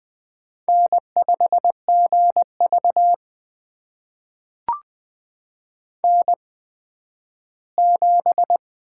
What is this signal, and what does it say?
Morse code, heard as a keyed tone, about 20 wpm, spaced as for 12 wpm, 700 Hz. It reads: N5GV N7
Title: USA Call Signs - 4 Characters - 8x Character Spacing - 20wpm